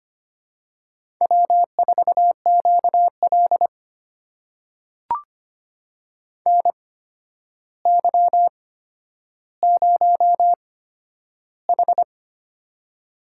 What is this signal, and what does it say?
Morse code, heard as a keyed tone, about 25 wpm, spaced as for 11 wpm, 700 Hz. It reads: W4QL NY0H
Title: USA Call Signs - 4 Characters - 8x Character Spacing - 25wpm